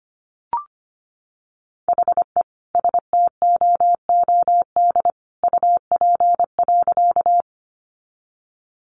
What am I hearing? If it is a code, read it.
Morse code, heard as a keyed tone, about 25 wpm, 700 Hz. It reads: HE STOOD UP.
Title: Sentences from Top 500 Words 25wpm